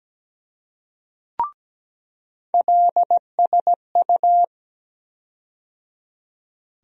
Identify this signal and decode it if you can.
Morse code, heard as a keyed tone, about 17 wpm, 700 Hz. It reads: LSU
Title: USA Call Sign Postfixes - 3 Letters 17wpm